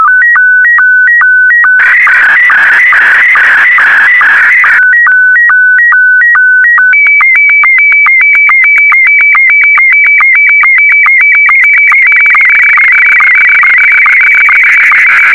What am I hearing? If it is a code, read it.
SSTV (slow-scan television), Scottie 1.